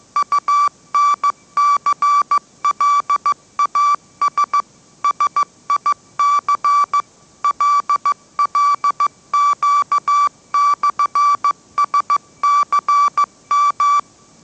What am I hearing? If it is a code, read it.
Morse code, heard as a keyed tone, about 15 wpm, 1160 Hz. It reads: UNCLAS SIC LLQ/SCM
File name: mrx / mrx066 16wpm